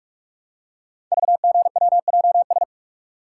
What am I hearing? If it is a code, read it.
Morse code, heard as a keyed tone, about 44 wpm, 700 Hz. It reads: VGWJS
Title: Code Groups - 5 Letters - 45wpm